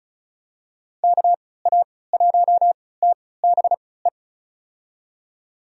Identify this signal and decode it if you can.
Morse code, heard as a keyed tone, about 36 wpm, spaced as for 20 wpm, 700 Hz. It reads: KA1TBE
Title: USA Call Signs - 6 Characters - 3x Character Spacing - 35wpm